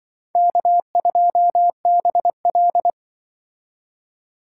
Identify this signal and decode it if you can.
Morse code, heard as a keyed tone, about 24 wpm, 700 Hz. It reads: K2BL
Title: Speed Racing - USA Call Signs 20wpm